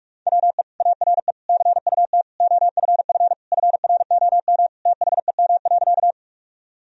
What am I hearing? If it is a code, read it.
Morse code, heard as a keyed tone, about 47 wpm, 700 Hz. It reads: WE ARE CUT OFF FROM THEM.
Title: Sentences from Top 1000 Words - Review Part 1 - 45wpm